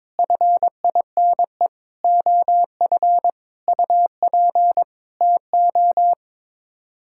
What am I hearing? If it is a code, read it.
Morse code, heard as a keyed tone, about 22 wpm, 700 Hz. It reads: FINE OF UP TO
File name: Sets of 9 Words - 22wpm